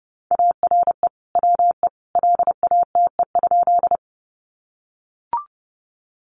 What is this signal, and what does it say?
Morse code, heard as a keyed tone, about 30 wpm, 700 Hz. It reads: ARE WE LATE?
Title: Sentences from Top 300 Words 30wpm